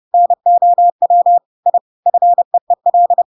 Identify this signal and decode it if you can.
Morse code, heard as a keyed tone, about 30 wpm, 700 Hz. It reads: NOW I FEEL
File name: Pinocchio / Pinocchio ch7 30 30 ik1hge